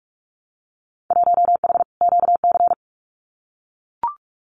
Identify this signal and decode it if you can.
Morse code, heard as a keyed tone, about 44 wpm, 700 Hz. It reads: 1H QC
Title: Winter Field Day Exchanges 45wpm